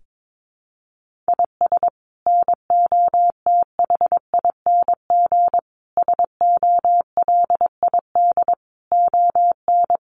Morse code, heard as a keyed tone, about 22 wpm, 700 Hz. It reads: IS NOTHING SOLID ON
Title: Sentences from Top 1500 Words 22wpm